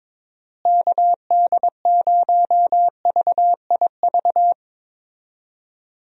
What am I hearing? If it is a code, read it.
Morse code, heard as a keyed tone, about 22 wpm, 700 Hz. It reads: KD0VIV
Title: USA Call Signs - 6 Characters - Repeated 3x - 22wpm